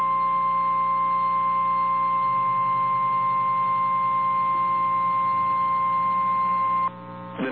Emergency Alert System (SAME) tones (the 1050 Hz attention tone), with noise. It is heard from 0.3 to 6.7 s.
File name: eas / KNCYKBIE / KNCYKBIE